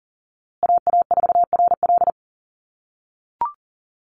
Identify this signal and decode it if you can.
Morse code, heard as a keyed tone, about 40 wpm, 700 Hz. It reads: AA4RL
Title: USA Call Signs Repeated 2x 40wpm